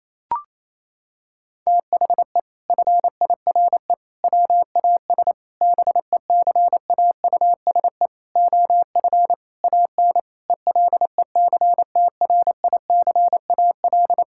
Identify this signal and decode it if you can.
Morse code, heard as a keyed tone, about 27 wpm, 700 Hz. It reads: THE FIRE WAS BECAUSE OF AN ELECTRICAL
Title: Sentences from Top 1000 Words - Review Part 6 - 28wpm